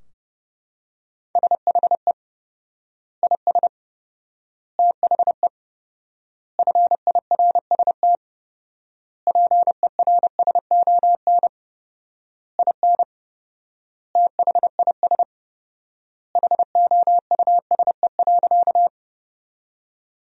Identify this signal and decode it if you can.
Morse code, heard as a keyed tone, about 32 wpm, 700 Hz. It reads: SHE IS THE FIRST PERSON IN THIS HOUSE.
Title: 4x Word-Spacing - Sentences from Top 400 Words - 30wpm